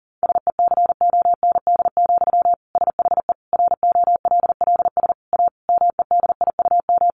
Morse code, heard as a keyed tone, about 40 wpm, 700 Hz. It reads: SECOND, SHE ROLLS A MEDIUM
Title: Easy Phrases 40wpm